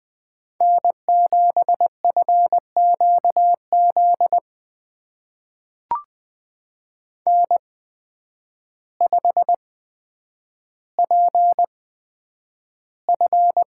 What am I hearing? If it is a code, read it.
Morse code, heard as a keyed tone, about 20 wpm, spaced as for 9 wpm, 700 Hz. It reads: N7FQZ N5PF
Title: USA Call Signs - 5 Characters - 8x Character Spacing - 20wpm